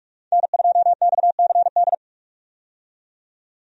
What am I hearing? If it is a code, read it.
Morse code, heard as a keyed tone, about 46 wpm, 700 Hz. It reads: N2XCD
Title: USA Call Signs - 5 Characters - 45wpm